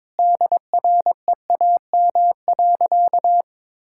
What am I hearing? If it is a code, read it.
Morse code, heard as a keyed tone, about 22 wpm, 700 Hz. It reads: DREAM.
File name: practice / 22wpm / Sentences with 2-Word Phrases - 22wpm